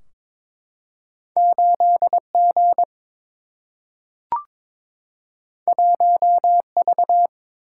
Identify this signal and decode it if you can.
Morse code, heard as a keyed tone, about 22 wpm, 700 Hz. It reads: 8 G 1 V
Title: Single-Letter Single-Number 22wpm